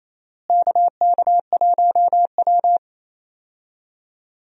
Morse code, heard as a keyed tone, about 28 wpm, 700 Hz. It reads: KK1W
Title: USA Call Signs - 4 Characters - Repeated 3x - 28wpm